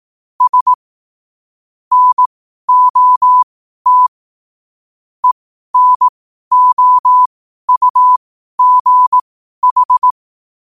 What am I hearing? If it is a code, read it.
Morse code, heard as a keyed tone, about 18 wpm, spaced as for 12 wpm, 1000 Hz. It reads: S NOT ENOUGH
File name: qotdfiles / 12wpm